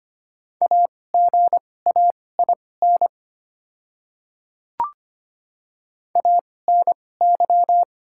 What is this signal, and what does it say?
Morse code, heard as a keyed tone, about 25 wpm, spaced as for 18 wpm, 690 Hz. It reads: AGAIN ANY
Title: 2x Character Spacing - Top 300 Words - 25wpm